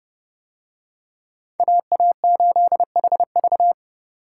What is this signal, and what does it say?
Morse code, heard as a keyed tone, about 30 wpm, 700 Hz. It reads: AA8HV